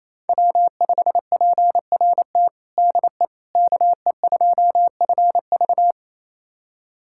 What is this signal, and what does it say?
Morse code, heard as a keyed tone, about 28 wpm, 690 Hz. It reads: W5PRT DE KE2FV
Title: USA Callsign DE Callsign - 5 Characters - 28wpm